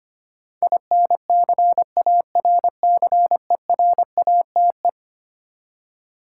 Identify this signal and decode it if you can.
Morse code, heard as a keyed tone, about 25 wpm, 700 Hz. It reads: INCARCERATE
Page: morse-code-ninja-practice-25wpm